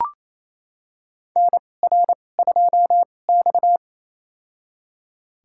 Morse code, heard as a keyed tone, about 28 wpm, spaced as for 20 wpm, 700 Hz. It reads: NR2X
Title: USA Call Signs - 4 Characters - 2x Character Spacing - 28wpm